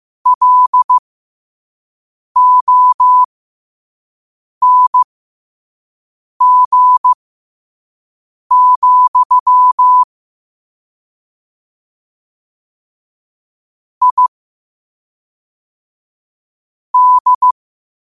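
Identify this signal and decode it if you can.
Morse code, heard as a keyed tone, about 15 wpm, spaced as for 5 wpm, 1000 Hz. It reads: LONG, I D
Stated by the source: Morse code practice at 5 wpm.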